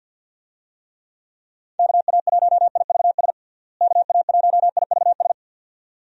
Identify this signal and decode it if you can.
Morse code, heard as a keyed tone, about 52 wpm, 700 Hz. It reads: KA1IVS KA1IVS
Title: USA Call Signs - 6 Characters - Repeated 2x - 50wpm